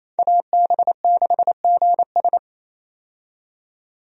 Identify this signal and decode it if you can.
Morse code, heard as a keyed tone, about 28 wpm, 700 Hz. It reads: AB6GS
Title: USA Call Signs - 5 Characters - 28wpm